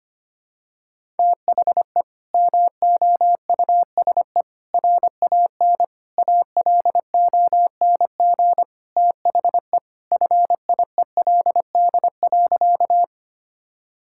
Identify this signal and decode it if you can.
Morse code, heard as a keyed tone, about 25 wpm, 700 Hz. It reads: THE MOUSE RAN ALONG THE FIELD.
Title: Sentences from Top 1000 Words - Review Part 5 - 25wpm